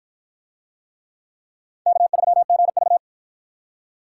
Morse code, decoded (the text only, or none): K3GV